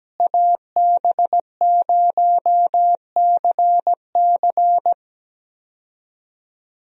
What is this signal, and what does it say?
Morse code, heard as a keyed tone, about 17 wpm, 700 Hz. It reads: AB0CC
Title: USA Call Signs - 5 Characters - Repeated 2x - 17wpm